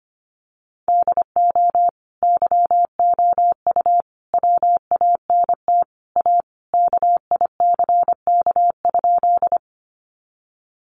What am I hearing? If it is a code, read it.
Morse code, heard as a keyed tone, about 25 wpm, 700 Hz. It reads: DO YOU WANT A KICK?
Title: Sentences from Top 1300 Words 25wpm